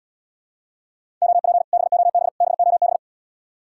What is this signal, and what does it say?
Morse code, heard as a keyed tone, about 80 wpm, 700 Hz. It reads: OP BOB BOB